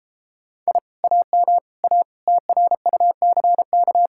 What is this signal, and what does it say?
Morse code, heard as a keyed tone, about 31 wpm, 700 Hz. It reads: I AM A TRUCK